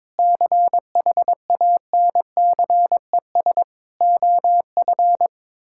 Morse code, heard as a keyed tone, about 22 wpm, 700 Hz. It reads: CHANCES OF
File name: practice / 22wpm / Sentences with 8-Word Phrases - 22wpm